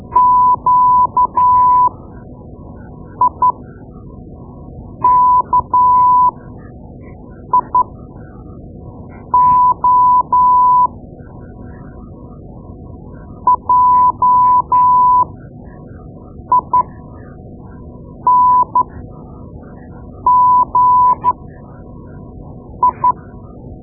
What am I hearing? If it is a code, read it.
Morse code, heard as a keyed tone, about 12 wpm, spaced as for 5 wpm, 1000 Hz. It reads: QIKIO JINGI
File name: Morse Lessons / Morse Lessons / Tape-2B